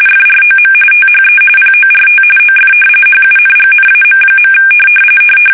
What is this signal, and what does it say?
RTTY, about 45.45 baud. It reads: FROM THE PORT OF HUNGNAM.
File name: fsk850 / usnavy5